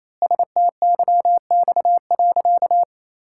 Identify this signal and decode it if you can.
Morse code, heard as a keyed tone, about 27 wpm, 700 Hz. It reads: STYX.
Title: Pangram Challenge v2 - 28wpm